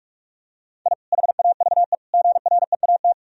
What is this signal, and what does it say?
Morse code, heard as a keyed tone, about 41 wpm, 700 Hz. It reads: I HAVE GREAT
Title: Sentences from Top 1000 Words - Review Part 2 - 45wpm